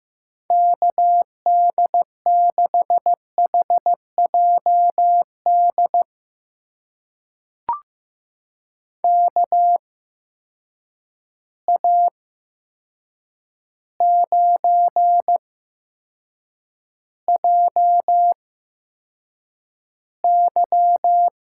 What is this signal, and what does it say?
Morse code, heard as a keyed tone, about 15 wpm, spaced as for 7 wpm, 700 Hz. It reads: KD6HJD KA9JY
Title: USA Call Signs - 8x Character Spacing - 15wpm